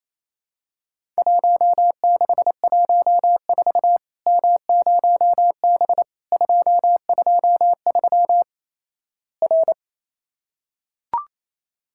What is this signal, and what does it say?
Morse code, heard as a keyed tone, about 28 wpm, 700 Hz. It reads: 1614 M0B 223 R
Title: Worked All Europe DX Contest - QTC Exchange - 28wpm